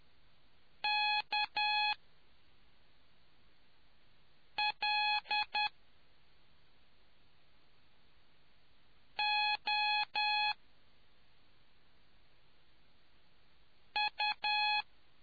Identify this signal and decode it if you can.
Morse code, heard as a keyed tone, about 10 wpm, spaced as for 2 wpm, 840 Hz. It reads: KLOU